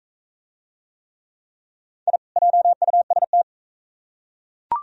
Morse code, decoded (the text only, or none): I JUST